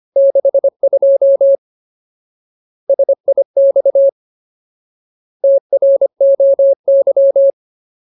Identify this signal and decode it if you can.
Morse code, heard as a keyed tone, about 25 wpm, 550 Hz. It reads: 62 SIX TROY
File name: Ultimate Word List / 25WPM Ultimate